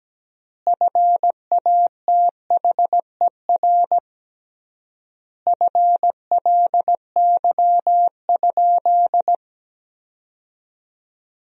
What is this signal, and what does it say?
Morse code, heard as a keyed tone, about 17 wpm, 700 Hz. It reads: FATHER FLY?
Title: 3x Word-Spacing - Sentences from Top 500 Words - 17wpm